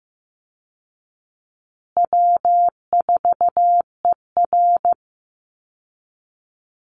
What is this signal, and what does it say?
Morse code, heard as a keyed tone, about 15 wpm, 700 Hz. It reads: W4ER
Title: Easy Call Signs Repeated 3x - 15wpm